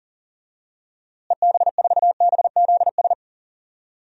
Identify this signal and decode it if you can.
Morse code, heard as a keyed tone, about 40 wpm, 700 Hz. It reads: EB4BZS
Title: DXCC-100 Call Signs - 6 Characters - Repeated 3x - 40wpm